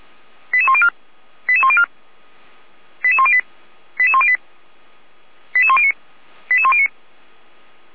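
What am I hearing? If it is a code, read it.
Five-tone selective call (ZVEI), heard as quick runs of five tones, tones of about 70 ms. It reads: 88104 88104 88108 88108 88109 88109
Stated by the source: ZVEI five-tone selective call received on SDR, from the Feuerwehr Orlamünde (fire brigade) audio server.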